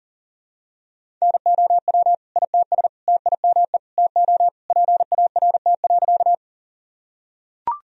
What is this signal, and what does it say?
Morse code, heard as a keyed tone, about 40 wpm, 700 Hz. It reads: NOW ITS TIME TO PART.
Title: Sentences from Top 1000 Words - Review Part 1 - 40wpm